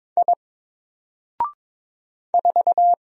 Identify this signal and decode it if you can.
Morse code, heard as a keyed tone, about 22 wpm, 700 Hz. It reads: I 4
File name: Single-Letter-Number-22wpm